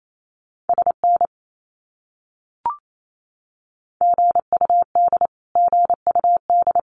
Morse code, heard as a keyed tone, about 28 wpm, 700 Hz. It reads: SN GUD GUD
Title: Abbreviations Repeated 3x - 28wpm